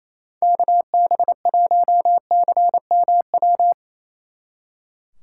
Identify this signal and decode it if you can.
Morse code, heard as a keyed tone, about 35 wpm, 700 Hz. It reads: KB1CMW
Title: Speed Racing - USA Call Signs 28wpm